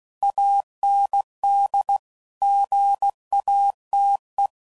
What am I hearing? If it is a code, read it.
Morse code, heard as a keyed tone, about 16 wpm, 800 Hz. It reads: AND GATE
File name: CW Code Groups @15WPM / Technical Subjects / AND Gate